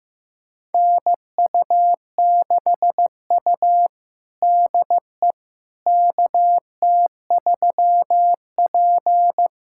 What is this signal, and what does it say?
Morse code, heard as a keyed tone, about 15 wpm, 700 Hz. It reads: NU6U DE KT3P
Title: USA Callsign DE Callsign - 4 Characters - 15wpm